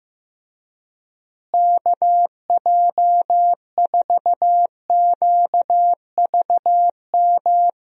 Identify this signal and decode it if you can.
Morse code, heard as a keyed tone, about 16 wpm, 700 Hz. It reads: KJ4QVM
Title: USA Call Signs - 6 Characters - Repeated 2x - 15wpm